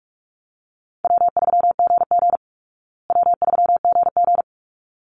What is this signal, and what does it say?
Morse code, heard as a keyed tone, about 44 wpm, 700 Hz. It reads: W3GG W3GG
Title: Easy Call Signs Repeated 2x - 45wpm